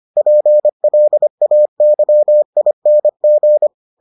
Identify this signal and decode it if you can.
Morse code, heard as a keyed tone, about 25 wpm, 600 Hz. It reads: PLAYING